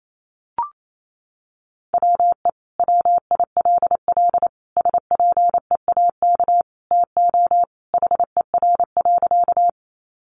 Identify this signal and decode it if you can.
Morse code, heard as a keyed tone, about 28 wpm, 700 Hz. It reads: WE WILL SPEAK TO HER.
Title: Sentences from Top 1100 Words 28wpm